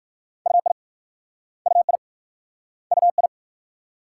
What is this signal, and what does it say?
Morse code, heard as a keyed tone, about 54 wpm, 700 Hz. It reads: UI UI UI
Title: Speed-Racing Multi-letter Phonemes - 45wpm